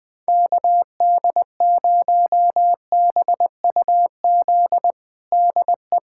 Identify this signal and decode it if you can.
Morse code, heard as a keyed tone, about 21 wpm, 700 Hz. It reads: KD0BUZ DE
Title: USA Callsign DE Callsign - 6 Characters - 20wpm